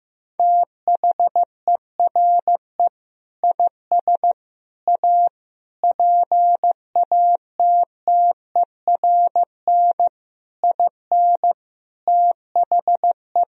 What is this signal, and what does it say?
Morse code, heard as a keyed tone, about 15 wpm, 700 Hz. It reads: THERE IS A PATTERN IN THE
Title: Sentences from Top 1000 Words - Review Part 2 - 15wpm